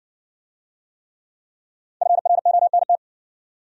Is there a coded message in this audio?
Yes: Morse code from 2.0 to 3.0 s, heard as a keyed tone, about 56 wpm, 700 Hz. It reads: FRONT